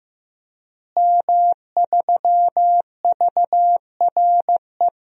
Morse code, heard as a keyed tone, about 15 wpm, 700 Hz. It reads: M3VRE
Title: DXCC-100 Call Signs - 5 Characters - Repeated 3x - 15wpm